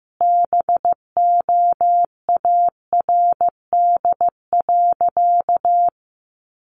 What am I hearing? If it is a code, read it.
Morse code, heard as a keyed tone, about 15 wpm, 700 Hz. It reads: BOARD.
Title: Sentences from Top 700 Words 15wpm